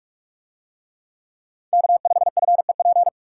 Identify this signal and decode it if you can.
Morse code, heard as a keyed tone, about 41 wpm, 700 Hz. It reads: K5FEP